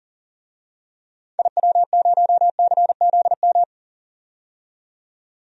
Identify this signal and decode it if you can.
Morse code, heard as a keyed tone, about 40 wpm, 700 Hz. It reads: IW0CZM
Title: DXCC-100 Call Signs - 6 Characters - 40wpm